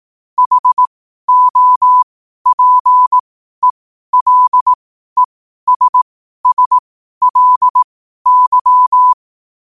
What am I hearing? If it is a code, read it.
Morse code, heard as a keyed tone, about 18 wpm, spaced as for 12 wpm, 1000 Hz. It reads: HOPELESSLY
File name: qotdfiles / 12wpm